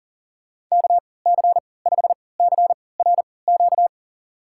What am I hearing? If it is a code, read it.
Morse code, heard as a keyed tone, about 40 wpm, spaced as for 23 wpm, 700 Hz. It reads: KC5CRQ
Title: USA Call Signs - 6 Characters - 3x Character Spacing - 40wpm